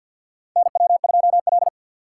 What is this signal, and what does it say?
Morse code, heard as a keyed tone, about 47 wpm, 700 Hz. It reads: NW2L
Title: USA Call Signs - 3x Character Spacing - 50wpm